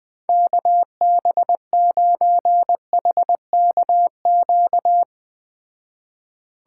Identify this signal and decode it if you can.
Morse code, heard as a keyed tone, about 20 wpm, 700 Hz. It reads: KB9HKQ